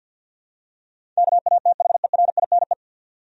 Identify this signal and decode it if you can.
Morse code, heard as a keyed tone, about 52 wpm, 700 Hz. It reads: KATHERINE
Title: USA Common Names - 50wpm